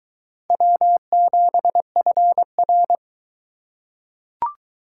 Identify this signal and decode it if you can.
Morse code, heard as a keyed tone, about 22 wpm, 700 Hz. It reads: W7FR